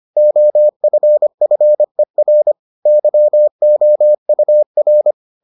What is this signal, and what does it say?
Morse code, heard as a keyed tone, about 25 wpm, 600 Hz. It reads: OFFER YOUR